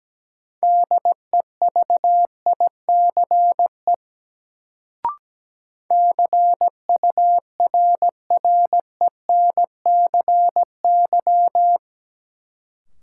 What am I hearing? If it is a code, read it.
Morse code, heard as a keyed tone, about 17 wpm, 700 Hz. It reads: DEVICE CURRENCY
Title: Sentences from Top 1500 Words - New Word Drill - 17wpm